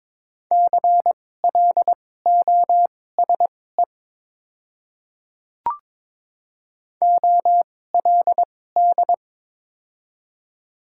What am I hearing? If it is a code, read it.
Morse code, heard as a keyed tone, about 22 wpm, spaced as for 16 wpm, 700 Hz. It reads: CLOSE OLD
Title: 2x Character Spacing - Top 500 Words - 22wpm